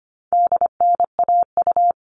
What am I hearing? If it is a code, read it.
Morse code, heard as a keyed tone, about 25 wpm, 700 Hz. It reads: DNAU
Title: All Cut Numbers - 4-Digit 25wpm